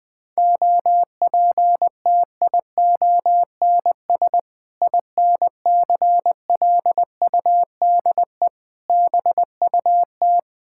Morse code, heard as a keyed tone, about 21 wpm, 700 Hz. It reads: OPTIONS INCLUDE BUT
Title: Sentences with 6-Word Phrases - 20wpm